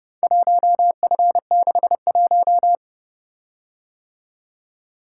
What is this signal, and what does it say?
Morse code, heard as a keyed tone, about 29 wpm, 700 Hz. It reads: 1F61